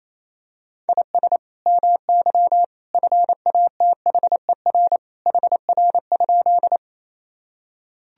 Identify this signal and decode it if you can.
Morse code, heard as a keyed tone, about 28 wpm, 700 Hz. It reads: IS MY FATHER HR?
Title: Sentences from Top 300 Words and Abbreviations 28wpm